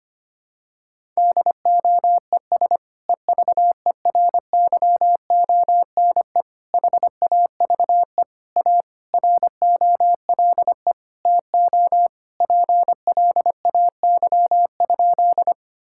Morse code, heard as a keyed tone, about 25 wpm, 700 Hz. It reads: DOES EVERYONE HAVE A ROLE TO PLAY?
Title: Sentences from Top 1000 Words - Review Part 6 - 25wpm